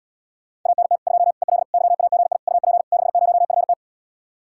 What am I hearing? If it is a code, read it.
Morse code, heard as a keyed tone, about 80 wpm, 700 Hz. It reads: ANT 2 EL YAGI UP 60FT